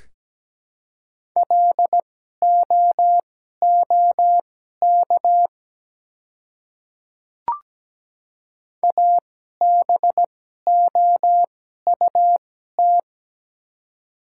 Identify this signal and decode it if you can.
Morse code, heard as a keyed tone, about 17 wpm, spaced as for 12 wpm, 690 Hz. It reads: LOOK ABOUT
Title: 2x Character Spacing - Top 100 Words - 17wpm